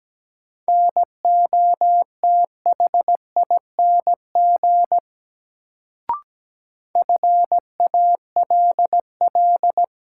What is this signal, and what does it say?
Morse code, heard as a keyed tone, about 17 wpm, 700 Hz. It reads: NOTHING FALL
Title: Sentences from Top 500 Words - New Word Drill - 17wpm